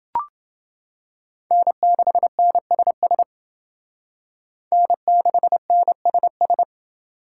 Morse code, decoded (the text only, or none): N6NSS N6NSS